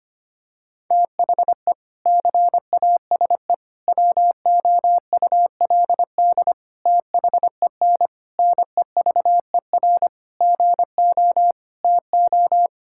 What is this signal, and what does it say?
Morse code, heard as a keyed tone, about 25 wpm, 700 Hz. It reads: THE CASE WOULD THEN NEVER GO TO
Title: Sentences from Top 1600 Words 25wpm